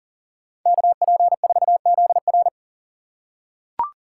Morse code, heard as a keyed tone, about 40 wpm, 700 Hz. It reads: KP4ZR
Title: USA Call Signs - 5 Characters - 3x Character Spacing - 40wpm